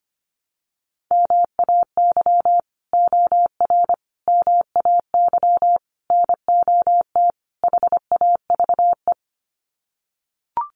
Morse code, heard as a keyed tone, about 25 wpm, 700 Hz. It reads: MAY OR MAY NOT HAVE
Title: Sets of 5 Words Encore2 25wpm